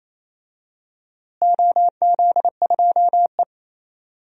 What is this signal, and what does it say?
Morse code, heard as a keyed tone, about 28 wpm, 700 Hz. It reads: OZ2E
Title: DXCC-100 Call Signs - 4 Characters - Repeated 3x - 28wpm